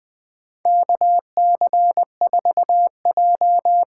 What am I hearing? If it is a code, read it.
Morse code, heard as a keyed tone, about 20 wpm, 700 Hz. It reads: KC4J
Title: USA Call Signs - 6 Characters - 8x Character Spacing - 20wpm